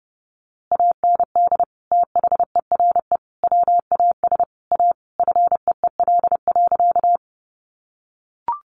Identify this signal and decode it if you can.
Morse code, heard as a keyed tone, about 30 wpm, 700 Hz. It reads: AND THERE WAS A FEEL.
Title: A Sound Of Thunder 30wpm